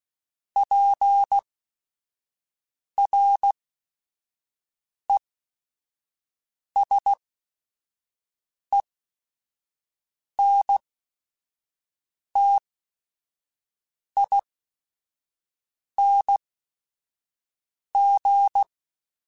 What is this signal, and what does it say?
Morse code, heard as a keyed tone, about 16 wpm, spaced as for 5 wpm, 800 Hz. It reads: PRESENTING
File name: Technical Subjects / Technical Subjects-mp3s / XC @5